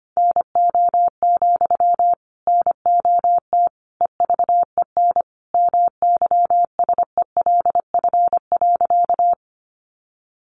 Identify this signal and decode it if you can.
Morse code, heard as a keyed tone, about 25 wpm, 700 Hz. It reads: NO, NOT EVEN MYSELF.
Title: Sentences from Top 1300 Words 25wpm